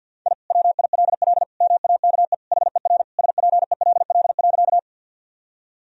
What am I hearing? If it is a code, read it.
Morse code, heard as a keyed tone, about 46 wpm, 700 Hz. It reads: I WILL MAKE HER SPELL.
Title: Sentences from Top 1000 Words - Review Part 1 - 50wpm